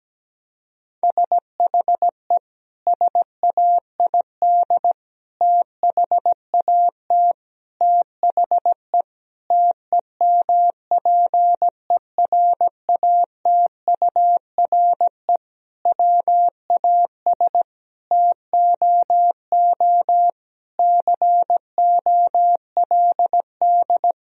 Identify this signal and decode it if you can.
Morse code, heard as a keyed tone, about 17 wpm, 700 Hz. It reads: SHE SAID THAT THE TEMPERATURE WAS TOO COLD
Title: Sentences from Top 1000 Words - Review Part 4 - 17wpm